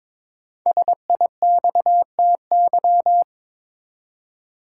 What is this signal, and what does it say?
Morse code, heard as a keyed tone, about 22 wpm, 700 Hz. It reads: SIXTY
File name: 5 Letter Words - First 3 Letters Given - 22wpm